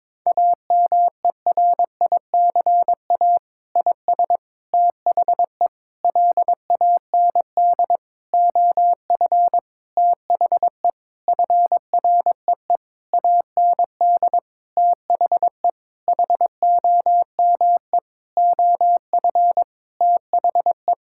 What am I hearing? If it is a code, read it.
Morse code, heard as a keyed tone, about 22 wpm, 700 Hz. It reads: AMERICA IS THE LAND OF THE FREE AND THE HOME OF THE
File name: practice / 22wpm / Sentences with 8-Word Phrases - 22wpm